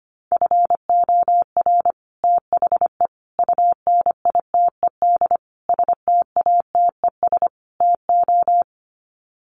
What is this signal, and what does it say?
Morse code, heard as a keyed tone, about 25 wpm, 700 Hz. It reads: FOR THE UNITED STATES TO
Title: Sets of 5 Words Encore 25wpm